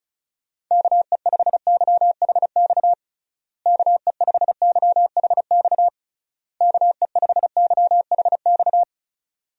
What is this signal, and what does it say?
Morse code, heard as a keyed tone, about 36 wpm, 700 Hz. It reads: KE5YHX KE5YHX KE5YHX
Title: USA Call Signs - 6 Characters - Repeated 3x - 35wpm